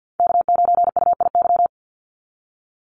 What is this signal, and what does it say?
Morse code, heard as a keyed tone, about 50 wpm, 700 Hz. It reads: K9UIY
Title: Call Signs Repeated 2x - 50wpm